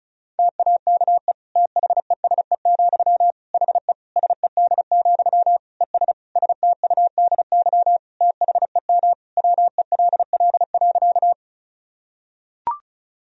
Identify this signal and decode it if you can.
Morse code, heard as a keyed tone, about 34 wpm, 700 Hz. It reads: TAKE THESE, HE SED, ES STUDY THEM WELL.
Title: Sentences from Top 300 Words and Abbreviations 35wpm